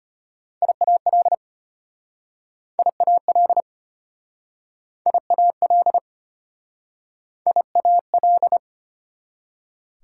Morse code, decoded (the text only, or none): IAL IAL IAL IAL